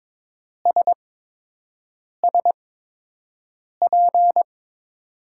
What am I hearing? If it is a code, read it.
Morse code, heard as a keyed tone, about 22 wpm, spaced as for 6 wpm, 700 Hz. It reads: SSP
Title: USA Call Signs - 8x Character Spacing - 22wpm